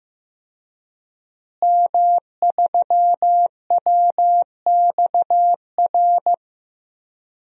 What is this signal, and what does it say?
Morse code, heard as a keyed tone, about 15 wpm, 700 Hz. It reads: M3WXR
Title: DXCC-100 Call Signs - 5 Characters - Repeated 3x - 15wpm